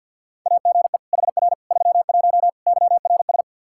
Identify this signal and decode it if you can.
Morse code, heard as a keyed tone, about 53 wpm, 700 Hz. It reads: AGE HR 31 YRS
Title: Practice QSO - Exchange 3 - More Info - 50wpm